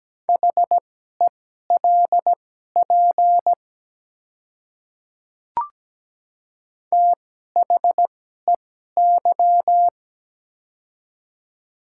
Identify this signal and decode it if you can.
Morse code, heard as a keyed tone, about 17 wpm, spaced as for 12 wpm, 700 Hz. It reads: HELP THEY
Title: 2x Character Spacing - Top 300 Words - 17wpm